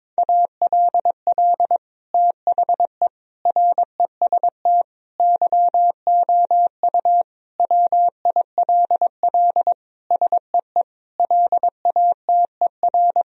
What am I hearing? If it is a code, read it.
Morse code, heard as a keyed tone, about 22 wpm, 700 Hz. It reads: ALL THE REST YOU WILL SEE LATER
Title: Sentences from Top 1000 Words - Review Part 4 - 22wpm